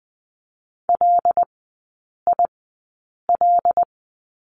Morse code, heard as a keyed tone, about 20 wpm, spaced as for 8 wpm, 700 Hz. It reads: LIL
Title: 074 Lesson 13e - Letter L - Three Characters 20wpm